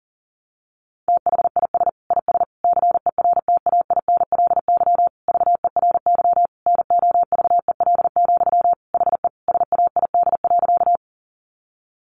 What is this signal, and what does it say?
Morse code, heard as a keyed tone, about 40 wpm, 700 Hz. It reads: THIS IS CERTAINLY VERY NOVEL, HE SAID.
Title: Sentences from Top 1300 Words 40wpm